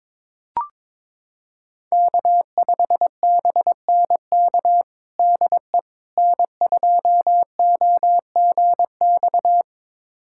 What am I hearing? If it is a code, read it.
Morse code, heard as a keyed tone, about 22 wpm, 700 Hz. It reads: K5BNK DE N2OGX
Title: USA Callsign DE Callsign - 5 Characters - 22wpm